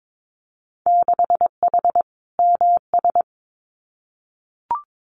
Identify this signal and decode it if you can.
Morse code, heard as a keyed tone, about 22 wpm, 700 Hz. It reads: 6H MS
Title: Winter Field Day Exchanges Encore 22wpm